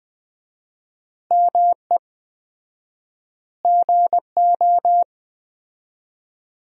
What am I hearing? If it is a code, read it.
Morse code, heard as a keyed tone, about 21 wpm, 700 Hz. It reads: ME GO
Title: 4x Word-Spacing - Sentences from Top 300 Words - 20wpm